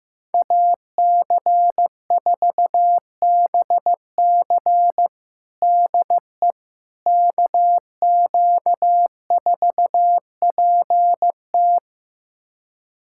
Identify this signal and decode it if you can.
Morse code, heard as a keyed tone, about 16 wpm, 700 Hz. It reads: AC4BC DE KQ4PT
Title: USA Callsign DE Callsign - 5 Characters - 15wpm